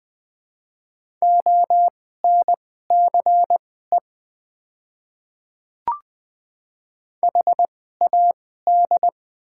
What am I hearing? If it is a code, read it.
Morse code, heard as a keyed tone, about 20 wpm, spaced as for 14 wpm, 700 Hz. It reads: ONCE HAD
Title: 2x Character Spacing - Top 300 Words - 20wpm